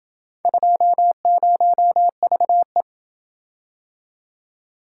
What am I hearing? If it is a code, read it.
Morse code, heard as a keyed tone, about 27 wpm, 700 Hz. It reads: 20VE